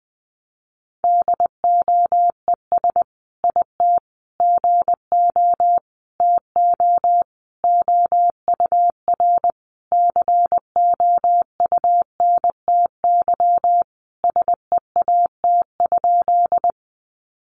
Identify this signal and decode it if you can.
Morse code, heard as a keyed tone, about 20 wpm, 700 Hz. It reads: DOES IT GO TO OUR COUNTY SEAT?
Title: Sentences from Top 1500 Words 20wpm